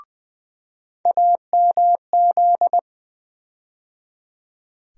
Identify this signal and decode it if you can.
Morse code, heard as a keyed tone, about 20 wpm, 700 Hz. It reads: AMZ
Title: USA Call Sign Postfixes - 3 Letters 20wpm